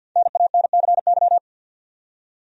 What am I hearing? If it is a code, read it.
Morse code, heard as a keyed tone, about 52 wpm, 700 Hz. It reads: NANCY
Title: USA Common Names - Max 5 Letters - 50wpm